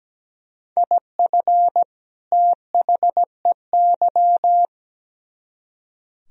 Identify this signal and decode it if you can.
Morse code, heard as a keyed tone, about 20 wpm, 700 Hz. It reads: IF THEY
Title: Speed-Racing - Sets of 2 Words 17wpm